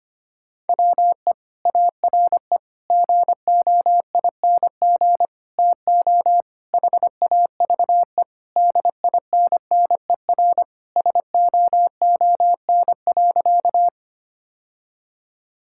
Morse code, heard as a keyed tone, about 25 wpm, 700 Hz. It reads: WE ARE GOING TO HAVE DINNER SOON.
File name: practice / 25wpm / Sentences with 3-Word Phrases - 25wpm